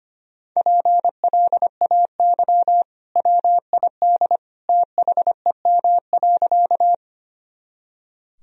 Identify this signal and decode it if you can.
Morse code, heard as a keyed tone, about 25 wpm, 700 Hz. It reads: PLAY WID THEM.
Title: Sentences from Top 200 Words and Abbreviations 25wpm